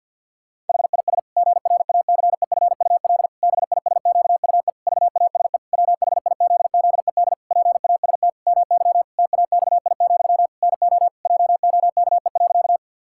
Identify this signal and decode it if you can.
Morse code, heard as a keyed tone, about 50 wpm, 700 Hz. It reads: HIS GRACEFUL BISQUE VASE WHIZZED PAST MY TAXI, NO JOKE.
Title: Pangram Challenge v2 - 50wpm